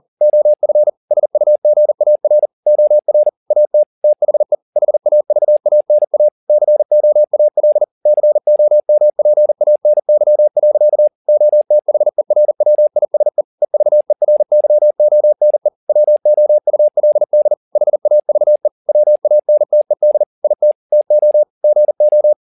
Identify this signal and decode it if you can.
Morse code, heard as a keyed tone, about 40 wpm, 600 Hz. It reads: OF SUGAR OR AT THE HAVANA COAL COMPANY. OTHERWISE EVERYONE WOULD HAVE WANTED IT TO GO
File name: the old man and the sea 2 40wpm